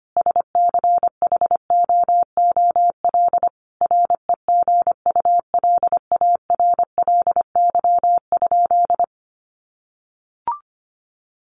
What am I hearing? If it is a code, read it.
Morse code, heard as a keyed tone, about 25 wpm, 700 Hz. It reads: SCHOOL REGULARLY?
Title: Sentences from Top 1400 Words 25wpm